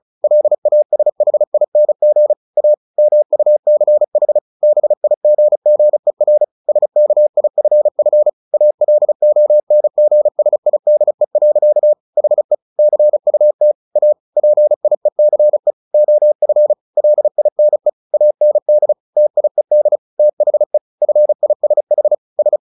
Morse code, heard as a keyed tone, about 34 wpm, 600 Hz. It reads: LASHING A MUCH BIGGER SKIFF ALONGSIDE. HE CUT A PIECE OF LINE AND TIED THE FISH S
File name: the old man and the sea 2 35wpm